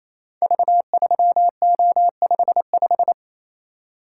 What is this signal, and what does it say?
Morse code, heard as a keyed tone, about 28 wpm, 700 Hz. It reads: V3O55
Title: Code Groups - 5 Letter:Numbers - 28wpm